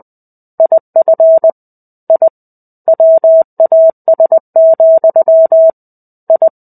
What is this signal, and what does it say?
Morse code, heard as a keyed tone, about 20 wpm, 650 Hz. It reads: IF I WAS, I
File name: 05-Chapter-05 20 WPM